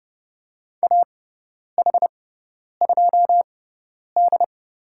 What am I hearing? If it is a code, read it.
Morse code, heard as a keyed tone, about 30 wpm, spaced as for 10 wpm, 700 Hz. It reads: AH2D
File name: Lesson 18g - Letter m - Call Signs 30wpm